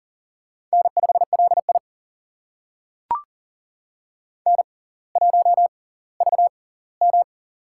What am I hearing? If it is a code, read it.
Morse code, heard as a keyed tone, about 39 wpm, spaced as for 20 wpm, 700 Hz. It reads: N5LI N1VM
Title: USA Call Signs - 4 Characters - 6x Character Spacing - 40wpm